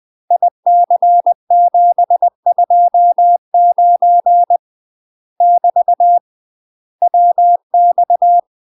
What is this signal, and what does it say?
Morse code, heard as a keyed tone, about 20 wpm, 700 Hz. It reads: IC729 <BT> WX
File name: UK MASTER QSOs TRAINING / 20WPM / UK MASTER QSOs 0002